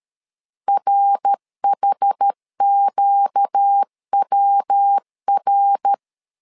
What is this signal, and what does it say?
Morse code, heard as a keyed tone, about 13 wpm, 800 Hz. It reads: RHQWR
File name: LEZIONI / 4 Q 12wpm 1833